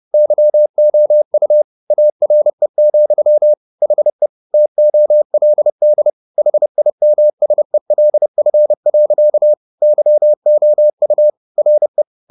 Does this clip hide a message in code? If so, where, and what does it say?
Yes: Morse code from 0.1 to 12.0 s, heard as a keyed tone, about 30 wpm, 600 Hz. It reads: YOU ARE, HE TOLD HIMSELF. YOU RE